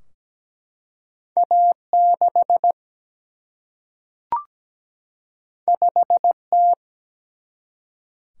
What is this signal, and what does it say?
Morse code, heard as a keyed tone, about 17 wpm, 700 Hz. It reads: A6 5T
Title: Single-Letter Single-Number 17wpm